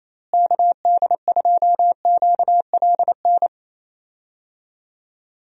Morse code, heard as a keyed tone, about 28 wpm, 700 Hz. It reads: KD2QLN
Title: USA Call Signs - 6 Characters - Repeated 2x - 28wpm